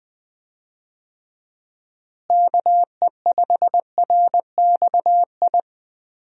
Morse code, heard as a keyed tone, about 20 wpm, 700 Hz. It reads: KE5RXI
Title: USA Call Signs - 6 Characters - Repeated 3x - 20wpm